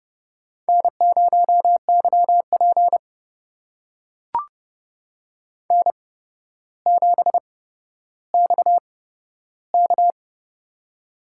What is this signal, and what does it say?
Morse code, heard as a keyed tone, about 31 wpm, spaced as for 13 wpm, 700 Hz. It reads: N0YP N7XK
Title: USA Call Signs - 4 Characters - 8x Character Spacing - 30wpm